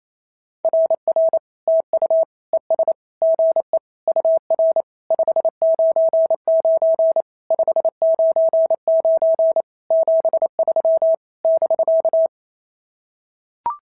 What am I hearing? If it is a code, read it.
Morse code, heard as a keyed tone, about 25 wpm, 650 Hz. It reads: RR TU ES GE UR 599 599 73 <BK>
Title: SOTA QSO Practice 28wpm